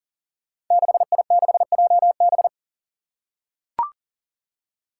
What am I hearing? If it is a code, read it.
Morse code, heard as a keyed tone, about 40 wpm, 700 Hz. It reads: 6I6JB
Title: Five Challenging Letter-Number Combinations 40wpm